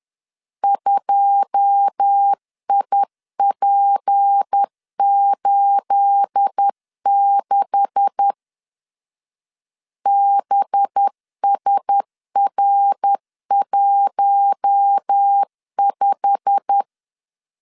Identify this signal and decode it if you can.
Morse code, heard as a keyed tone, about 10 wpm, 800 Hz. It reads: 2IP86 BSR15
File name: CW / DET / 1 9 1500C S A 10wpm 4511